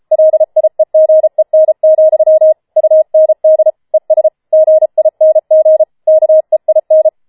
Morse code, heard as a keyed tone, about 33 wpm, 630 Hz. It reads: LIEGEN, UND ES GING KEIN
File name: Schneewittchen Teil5 34wpm